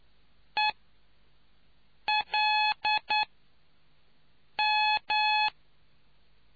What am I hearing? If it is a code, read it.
Morse code, heard as a keyed tone, about 9 wpm, 850 Hz. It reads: E L M